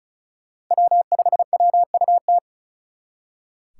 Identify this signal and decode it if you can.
Morse code, heard as a keyed tone, about 45 wpm, 700 Hz. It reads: W5WUT